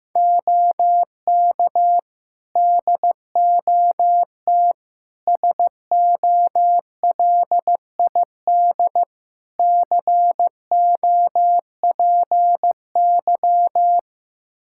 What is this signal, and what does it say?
Morse code, heard as a keyed tone, about 15 wpm, 700 Hz. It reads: OK DOT SOLID COPY
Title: Practice QSO - Exchange 3 - More Info - 15wpm